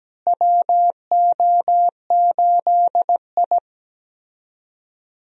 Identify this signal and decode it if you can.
Morse code, heard as a keyed tone, about 17 wpm, 700 Hz. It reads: WO8I
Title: USA Call Signs - 4 Characters - Repeated 3x - 17wpm